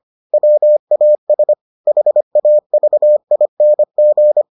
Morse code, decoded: WAS HAVING

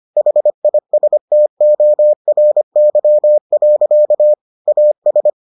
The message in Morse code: HISTORY. AS

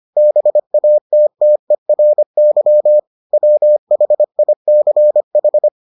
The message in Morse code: BATTERY WHICH

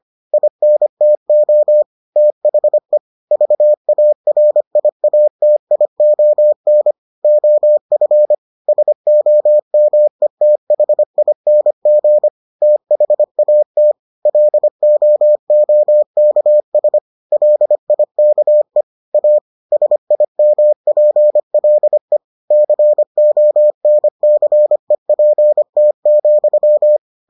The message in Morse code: INTO THE VARIATION OF SOMETHING THAT LOOKS LIKE A SIMPLE CONCEPT,

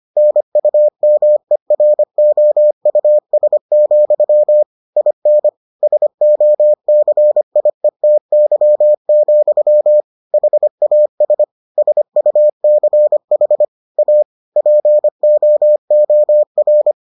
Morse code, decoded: NUMEROUS, IN SOCIETY, HAS SUCH A POOR

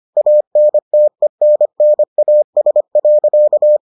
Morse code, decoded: ANTENNAS.

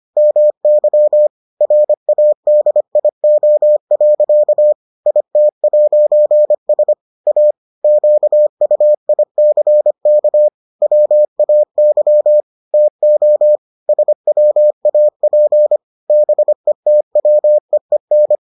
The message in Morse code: MY RADIO. IT'S A QUICK WAY TO SWAP BETWEEN

